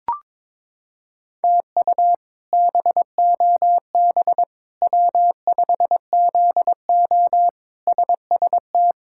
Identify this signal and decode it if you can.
Morse code, heard as a keyed tone, about 22 wpm, 700 Hz. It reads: TU BOB W5ZO SST
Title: K1USN SST QSO Practice - 22wpm